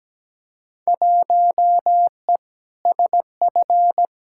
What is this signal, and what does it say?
Morse code, heard as a keyed tone, about 17 wpm, 700 Hz. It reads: 1E SF
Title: Field Day Exchanges - 17wpm